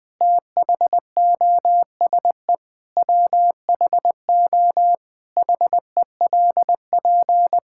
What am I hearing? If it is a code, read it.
Morse code, heard as a keyed tone, about 20 wpm, 700 Hz. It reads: THOSE WHO HELP
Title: Sentences from Top 1000 Words - Review Part 5 - 20wpm